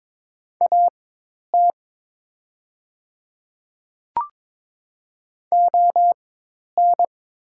Morse code, heard as a keyed tone, about 22 wpm, spaced as for 10 wpm, 700 Hz. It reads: AT ON
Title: 4x Character Spacing - Top 200 Words - 22wpm